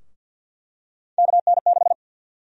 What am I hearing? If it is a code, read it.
Morse code, heard as a keyed tone, about 50 wpm, 700 Hz. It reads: KN6
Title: USA Call Sign Prefixes - Repeated 2x - 50wpm